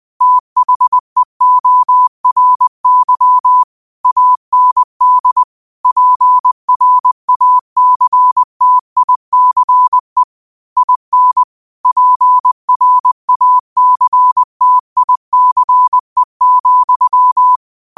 Morse code, heard as a keyed tone, about 20 wpm, 1000 Hz. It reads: THEORY AND PRACTICE IN PRACTICE,
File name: 20wpm